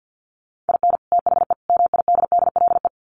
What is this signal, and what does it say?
Morse code, heard as a keyed tone, about 49 wpm, 700 Hz. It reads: IN THE MIDDLE
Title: Sets of 3 Words 50wpm